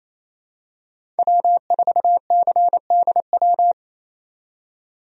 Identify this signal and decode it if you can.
Morse code, heard as a keyed tone, about 28 wpm, 700 Hz. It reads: W4CDW
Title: USA Call Signs - 5 Characters - 28wpm